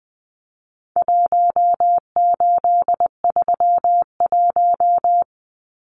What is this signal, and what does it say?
Morse code, heard as a keyed tone, about 20 wpm, 700 Hz. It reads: 1831
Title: Numbers - Contest Style - 4-Digit 20wpm